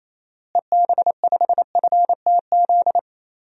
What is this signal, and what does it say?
Morse code, heard as a keyed tone, about 27 wpm, 700 Hz. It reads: EB5FTZ
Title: DXCC-100 Call Signs - 6 Characters - Repeated 2x - 28wpm